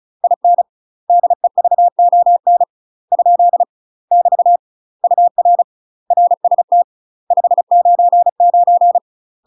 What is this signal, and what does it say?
Morse code, heard as a keyed tone, about 34 wpm, 700 Hz. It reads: IN DEVON ? <BT> UR RST 599
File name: UK MASTER QSOs TRAINING / 35WPM / UK MASTER QSOs 0018